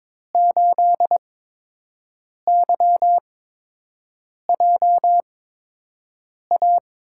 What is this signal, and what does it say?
Morse code, heard as a keyed tone, about 22 wpm, spaced as for 6 wpm, 700 Hz. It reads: 8YJA
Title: USA Call Signs - 8x Character Spacing - 22wpm